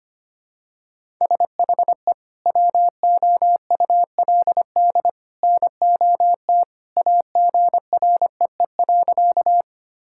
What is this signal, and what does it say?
Morse code, heard as a keyed tone, about 25 wpm, 700 Hz. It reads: SHE WOULD NOT AGREE.
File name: practice / 25wpm / Sentences with 2-Word Phrases - 25wpm